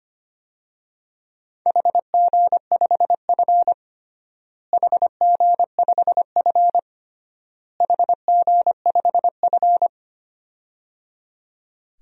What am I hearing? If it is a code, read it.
Morse code, heard as a keyed tone, about 25 wpm, 700 Hz. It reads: HG5F HG5F HG5F
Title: DXCC-100 Call Signs - 4 Characters - Repeated 3x - 25wpm